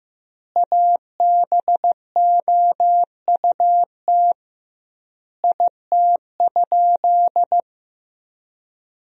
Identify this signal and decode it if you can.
Morse code, heard as a keyed tone, about 16 wpm, 700 Hz. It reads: ABOUT IT?
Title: 2x Word-Spacing - Sentences from Top 100 Words - 15wpm